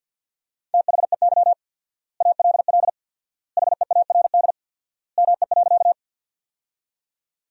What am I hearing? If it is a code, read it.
Morse code, heard as a keyed tone, about 52 wpm, 700 Hz. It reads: THEY ALL HEARD ME.
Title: 4x Word-Spacing - Sentences from Top 400 Words - 50wpm